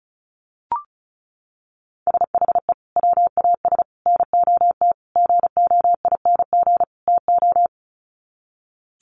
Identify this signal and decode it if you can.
Morse code, heard as a keyed tone, about 34 wpm, 700 Hz. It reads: SHE WAS NOT GOING TO
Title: Sets of 5 Words Encore2 35wpm